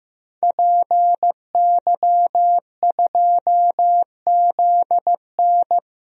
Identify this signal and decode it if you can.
Morse code, heard as a keyed tone, about 15 wpm, 700 Hz. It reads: PY2ZN
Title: DXCC-100 Call Signs - 5 Characters - Repeated 3x - 15wpm